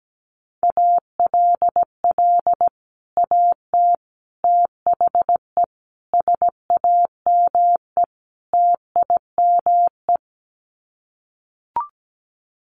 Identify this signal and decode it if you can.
Morse code, heard as a keyed tone, about 17 wpm, 700 Hz. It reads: ALL AT THE SAME TIME
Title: Sets of 5 Words Encore2 17wpm